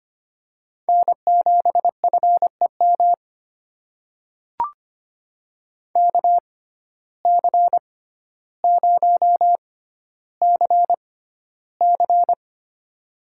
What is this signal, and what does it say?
Morse code, heard as a keyed tone, about 25 wpm, spaced as for 13 wpm, 700 Hz. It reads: N7FEM KC0CC
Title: USA Call Signs - 5 Characters - 6x Character Spacing - 25wpm